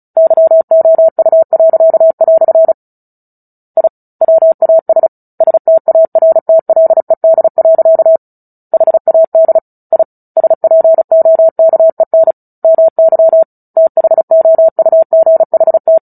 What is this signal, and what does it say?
Morse code, heard as a keyed tone, about 34 wpm, 650 Hz. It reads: YOU." I WAS STARTLED. HAD I SPOKEN MY THOUGHT
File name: Book Two / 09-Chapter 9 - 35 WPM